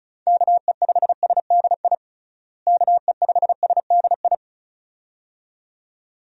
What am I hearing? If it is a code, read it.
Morse code, heard as a keyed tone, about 36 wpm, 700 Hz. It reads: KE5SDI KE5SDI
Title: USA Call Signs - 6 Characters - Repeated 2x - 35wpm